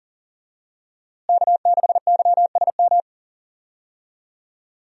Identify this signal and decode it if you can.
Morse code, heard as a keyed tone, about 40 wpm, 700 Hz. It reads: K6YSM
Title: USA Call Signs - 5 Characters - Repeated 3x - 40wpm